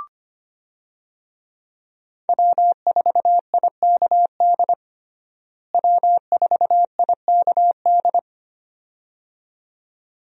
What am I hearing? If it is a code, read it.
Morse code, heard as a keyed tone, about 25 wpm, 700 Hz. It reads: W4IKD W4IKD
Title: USA Call Signs - 5 Characters - Repeated 2x - 25wpm